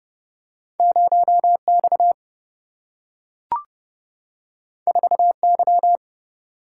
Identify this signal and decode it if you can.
Morse code, heard as a keyed tone, about 30 wpm, 700 Hz. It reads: 0X 4Y
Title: Single-Letter Single-Number 30wpm